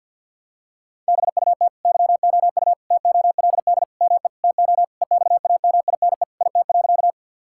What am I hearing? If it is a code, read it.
Morse code, heard as a keyed tone, about 50 wpm, 700 Hz. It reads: BUT YOU TOLD ME TO EXAMINE IT.
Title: Sentences from Top 1600 Words 50wpm